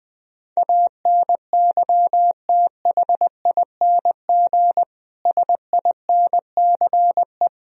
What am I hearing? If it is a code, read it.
Morse code, heard as a keyed tone, about 20 wpm, 700 Hz. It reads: ANYTHING SINCE
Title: Sentences from Top 1000 Words - Review Part 1 - 20wpm